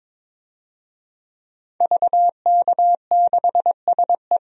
Morse code, heard as a keyed tone, about 22 wpm, 700 Hz. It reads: VK6SE
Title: DXCC-100 Call Signs - 5 Characters - 22wpm